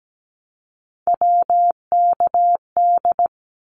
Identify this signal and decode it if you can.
Morse code, heard as a keyed tone, about 17 wpm, 700 Hz. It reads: WKD
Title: Abbreviations 17wpm